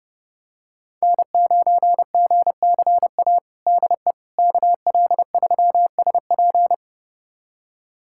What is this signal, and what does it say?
Morse code, heard as a keyed tone, about 30 wpm, 700 Hz. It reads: N9GCA DE KL3SP
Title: USA Callsign DE Callsign - 5 Characters - 30wpm